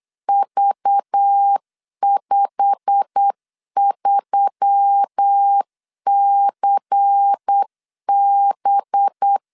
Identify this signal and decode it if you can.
Morse code, heard as a keyed tone, about 8 wpm, 800 Hz. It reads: V53CB
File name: CW / DET / 1 9 1500C S A 8wpm 5637